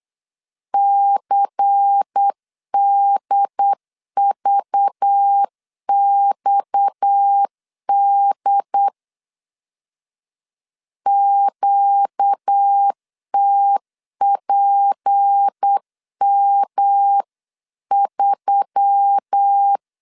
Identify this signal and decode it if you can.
Morse code, heard as a keyed tone, about 8 wpm, 800 Hz. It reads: CDVXD QTPM3
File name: CW / DET / 1 6 1500C S B 8wpm 5142